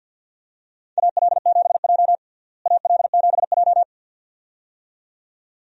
Morse code, heard as a keyed tone, about 53 wpm, 700 Hz. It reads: AL7J AL7J
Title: USA Call Signs - 4 Characters - Repeated 2x - 50wpm